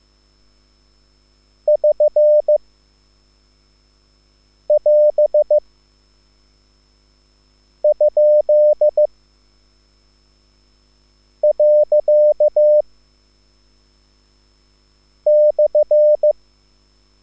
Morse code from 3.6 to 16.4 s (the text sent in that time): <AS> ? . /